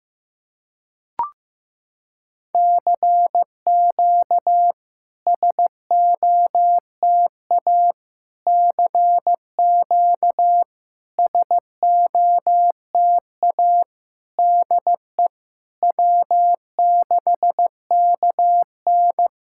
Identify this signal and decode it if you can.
Morse code, heard as a keyed tone, about 15 wpm, 700 Hz. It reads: CQ SOTA CQ SOTA DE W6KN
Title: SOTA QSO Practice 15wpm